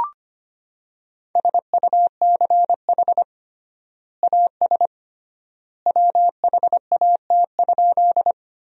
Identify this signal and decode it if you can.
Morse code, heard as a keyed tone, about 25 wpm, 700 Hz. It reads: SUCH AS WHAT?
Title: 3x Word-Spacing - Sentences from Top 200 Words - 25wpm